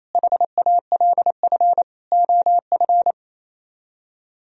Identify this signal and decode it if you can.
Morse code, heard as a keyed tone, about 27 wpm, 700 Hz. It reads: HALF OF
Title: Speed-Racing - Sets of 5 Words 28wpm